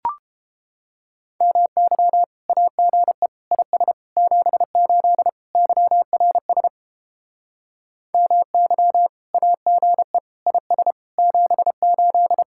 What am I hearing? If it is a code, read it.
Morse code, heard as a keyed tone, about 29 wpm, 700 Hz. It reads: MY AGE IS 78 YRS MY AGE IS 78